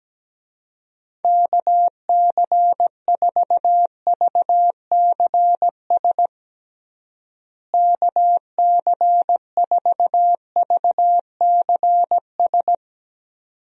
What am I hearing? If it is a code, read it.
Morse code, heard as a keyed tone, about 17 wpm, 700 Hz. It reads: KC4VCS KC4VCS
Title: USA Call Signs - 6 Characters - Repeated 2x - 17wpm